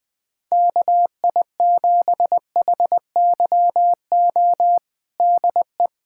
Morse code, heard as a keyed tone, about 21 wpm, 700 Hz. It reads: KI7HYO DE
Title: USA Callsign DE Callsign - 6 Characters - 20wpm